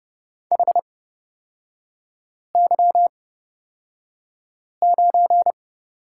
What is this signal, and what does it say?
Morse code, heard as a keyed tone, about 31 wpm, spaced as for 5 wpm, 700 Hz. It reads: HY9
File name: Lesson 37e - Letter x - Three Characters 30wpm